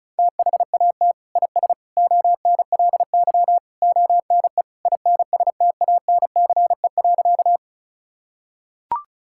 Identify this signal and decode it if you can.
Morse code, heard as a keyed tone, about 36 wpm, 700 Hz. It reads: THAT IS ONLY ONE INSTANCE.
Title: Sentences from Top 1000 Words - Review Part 5 - 35wpm